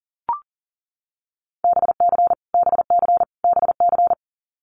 Morse code, heard as a keyed tone, about 40 wpm, 700 Hz. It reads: BC BC BC
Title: Canadian Provinces and Territories - Repeated 3x - 40wpm